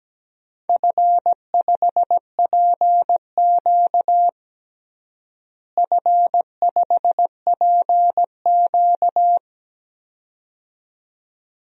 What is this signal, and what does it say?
Morse code, heard as a keyed tone, about 17 wpm, 700 Hz. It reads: F5PQ F5PQ
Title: DXCC-100 Call Signs - 4 Characters - Repeated 2x - 17wpm